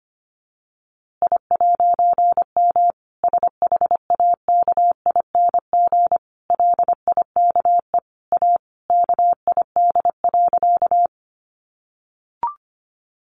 Morse code, heard as a keyed tone, about 25 wpm, 700 Hz. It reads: I'M SHAKING LIKE A KID.
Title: A Sound Of Thunder 25wpm